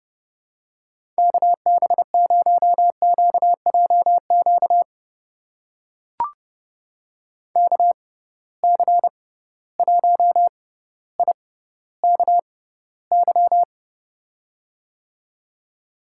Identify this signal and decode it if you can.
Morse code, heard as a keyed tone, about 30 wpm, spaced as for 15 wpm, 700 Hz. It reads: KB0QJQ KC1IKY
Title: USA Call Signs - 6x Character Spacing - 30wpm